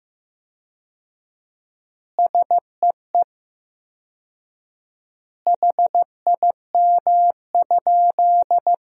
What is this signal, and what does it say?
Morse code, heard as a keyed tone, about 15 wpm, 700 Hz. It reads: SEE HIM?
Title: 4x Word-Spacing - Sentences from Top 100 Words - 15wpm